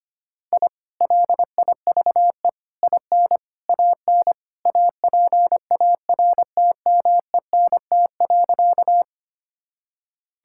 Morse code, heard as a keyed tone, about 25 wpm, 700 Hz. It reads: I LIVE IN AN APARTMENT.
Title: Sentences from Top 1000 Words - Review Part 6 - 25wpm